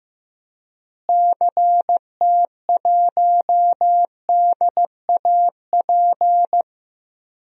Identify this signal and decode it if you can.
Morse code, heard as a keyed tone, about 15 wpm, 700 Hz. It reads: CT1DAP
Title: DXCC-100 Call Signs - 6 Characters - Repeated 3x - 15wpm